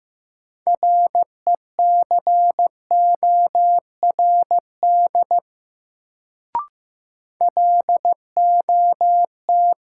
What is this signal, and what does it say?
Morse code, heard as a keyed tone, about 15 wpm, 700 Hz. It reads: RECORD LOT
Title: Sentences from Top 500 Words - New Word Drill - 15wpm